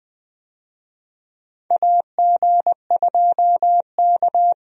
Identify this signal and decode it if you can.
Morse code, heard as a keyed tone, about 20 wpm, 700 Hz. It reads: AG2K
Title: USA Call Signs - 4 Characters - Repeated 3x - 20wpm